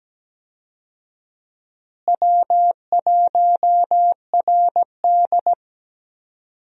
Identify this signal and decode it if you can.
Morse code, heard as a keyed tone, about 17 wpm, 700 Hz. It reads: W1RD
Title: USA Call Signs - 4 Characters - Repeated 2x - 17wpm